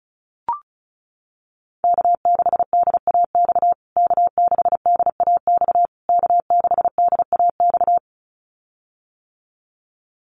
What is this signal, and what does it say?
Morse code, heard as a keyed tone, about 34 wpm, 700 Hz. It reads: K6DAX K6DAX K6DAX
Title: USA Call Signs Repeated 3x 35wpm